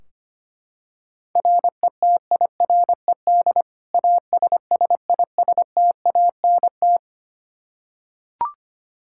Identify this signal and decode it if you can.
Morse code, heard as a keyed tone, about 25 wpm, 700 Hz. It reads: RETIRED ASSISTANT
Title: Practice QSO - Exchange 3 - More Info - 25wpm